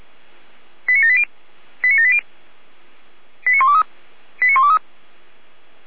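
Five-tone selective call (ZVEI), heard as quick runs of five tones, tones of about 70 ms. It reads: 89780 89780 89123 89123
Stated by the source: ZVEI five-tone selective call received on an SDR, from the audio archive of Feuerwehr Orlamünde.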